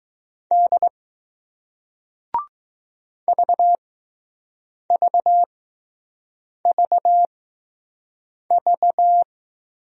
Morse code, heard as a keyed tone, about 20 wpm, spaced as for 6 wpm, 700 Hz. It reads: D VVVV